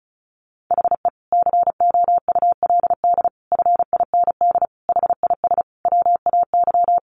Morse code, heard as a keyed tone, about 34 wpm, 700 Hz. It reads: HE COULD FIND HIS WAY
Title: Sentences from Top 1400 Words 35wpm